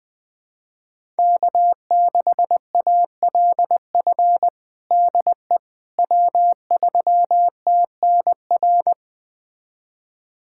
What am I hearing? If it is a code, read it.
Morse code, heard as a keyed tone, about 20 wpm, 700 Hz. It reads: K6ALF DE W3TNR
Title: USA Callsign DE Callsign - 5 Characters - 20wpm